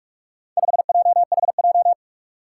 Morse code, heard as a keyed tone, about 44 wpm, 700 Hz. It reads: 5JHJ
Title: Four Challenging Letter-Number Combinations 45wpm